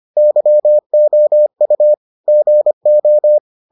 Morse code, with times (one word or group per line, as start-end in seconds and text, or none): 0.2-1.9 YOU
2.3-3.4 GO